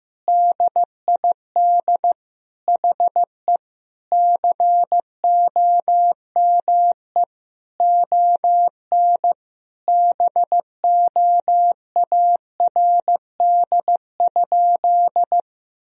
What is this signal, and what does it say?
Morse code, heard as a keyed tone, about 15 wpm, 700 Hz. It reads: DID HE COME ON BOARD?
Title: Sentences from Top 1000 Words - Review Part 4 - 15wpm